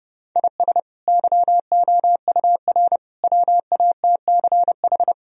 Morse code, heard as a keyed tone, about 30 wpm, 700 Hz. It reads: IS YOUR WATCH
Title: Sentences from Top 1000 Words - Review Part 2 - 30wpm